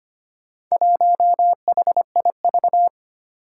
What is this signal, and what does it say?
Morse code, heard as a keyed tone, about 25 wpm, 700 Hz. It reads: 1HIV
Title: Four Challenging Letter-Number Combinations 25wpm